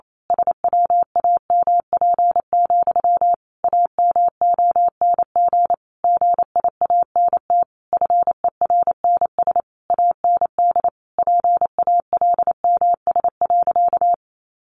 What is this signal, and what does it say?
Morse code, heard as a keyed tone, about 28 wpm, 700 Hz. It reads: SWAMP, AMONG GIANT FERNS AND PALMS.
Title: A Sound Of Thunder 28wpm